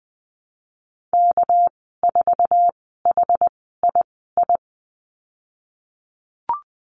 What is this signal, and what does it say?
Morse code, heard as a keyed tone, about 20 wpm, spaced as for 15 wpm, 700 Hz. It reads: K4HII
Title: 200 Lesson 32g - Letter k - Call Signs 20wpm